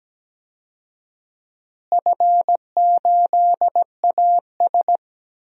Morse code, heard as a keyed tone, about 17 wpm, 700 Hz. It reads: F8AS
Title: DXCC-100 Call Signs - 4 Characters - Repeated 3x - 17wpm